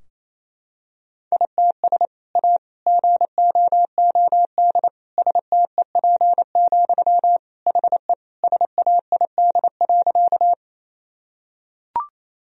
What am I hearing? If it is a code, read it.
Morse code, heard as a keyed tone, about 27 wpm, 700 Hz. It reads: ITS A GOOD STEP, HE SAID.
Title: Sentences from Top 1000 Words - Review Part 2 - 28wpm